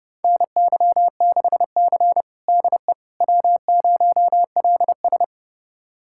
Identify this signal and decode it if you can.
Morse code, heard as a keyed tone, about 31 wpm, 700 Hz. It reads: NY6C DE W0LS
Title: USA Callsign DE Callsign - 4 Characters - 30wpm